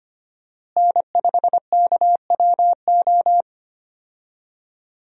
Morse code, heard as a keyed tone, about 25 wpm, 700 Hz. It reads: N5KWO
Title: USA Call Signs - 5 Characters - Repeated 3x - 25wpm